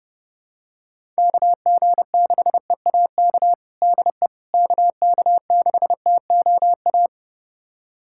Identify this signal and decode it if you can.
Morse code, heard as a keyed tone, about 32 wpm, 700 Hz. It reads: KG6EAK DE KK6TOA
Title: USA Callsign DE Callsign - 6 Characters - 30wpm